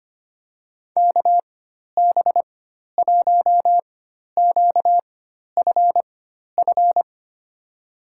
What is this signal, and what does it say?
Morse code, heard as a keyed tone, about 25 wpm, spaced as for 12 wpm, 700 Hz. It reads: KB1QFF
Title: USA Call Signs - 6 Characters - 4x Character Spacing - 25wpm